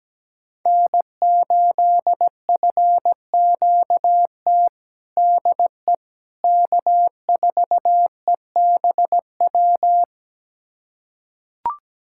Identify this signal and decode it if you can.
Morse code, heard as a keyed tone, about 17 wpm, 700 Hz. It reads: N8FQT DE K4EBW
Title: USA Callsign DE Callsign - 5 Characters - 17wpm